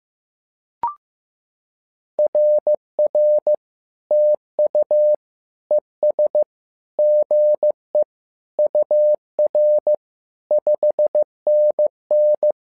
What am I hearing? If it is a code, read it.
Morse code, heard as a keyed tone, about 15 wpm, 600 Hz. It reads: RR TU ES GE UR 5NN
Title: SOTA QSO Practice 15wpm